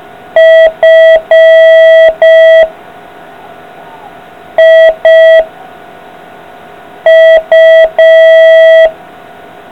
Morse code, heard as a keyed tone, about 5 wpm, 650 Hz. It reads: F I U